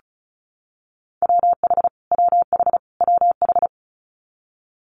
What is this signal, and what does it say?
Morse code, heard as a keyed tone, about 36 wpm, 700 Hz. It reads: WH WH WH
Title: Multi-letter Phonemes Repeated 3x 35wpm